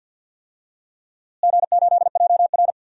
Morse code, heard as a keyed tone, about 51 wpm, 700 Hz. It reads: G8JR
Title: DXCC-100 Call Signs - 4 Characters - Repeated 2x - 50wpm